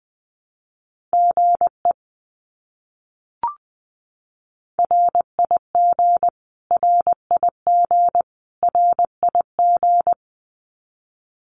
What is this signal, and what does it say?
Morse code, heard as a keyed tone, about 20 wpm, 700 Hz. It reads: GE RIG RIG RIG
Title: Abbreviations Repeated 3x - 20wpm